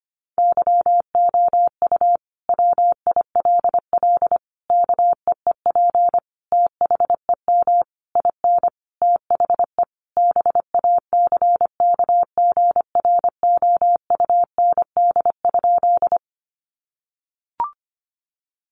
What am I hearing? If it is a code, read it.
Morse code, heard as a keyed tone, about 25 wpm, 700 Hz. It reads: YOU WILL KEEP THEM IN THE BACKGROUND?
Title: Sentences from Top 1300 Words 25wpm